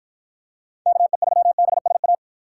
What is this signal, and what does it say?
Morse code, heard as a keyed tone, about 52 wpm, 700 Hz. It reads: KE3BSA